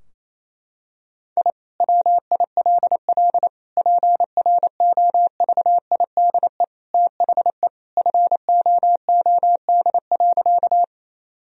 Morse code, heard as a keyed tone, about 28 wpm, 700 Hz. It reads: I WILL PROVIDE THE FOOD.
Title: Sentences from Top 1000 Words - Review Part 3 - 28wpm